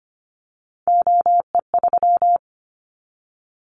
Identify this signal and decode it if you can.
Morse code, heard as a keyed tone, about 25 wpm, 700 Hz. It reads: OE3
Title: Callsign Prefixes for Top 25 DXCC - 25wpm